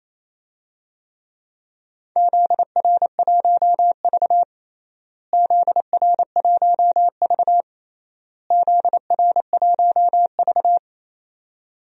Morse code, heard as a keyed tone, about 27 wpm, 700 Hz. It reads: ZR1V ZR1V ZR1V
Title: DXCC-100 Call Signs - 4 Characters - Repeated 3x - 28wpm